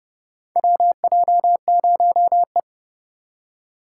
Morse code, heard as a keyed tone, about 30 wpm, 700 Hz. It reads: WJ0E
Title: USA Call Signs - 4 Characters - 2x Character Spacing - 30wpm